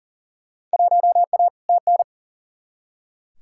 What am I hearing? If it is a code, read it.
Morse code, heard as a keyed tone, about 40 wpm, 700 Hz. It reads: 1A TN
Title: Field Day Exchanges - 40wpm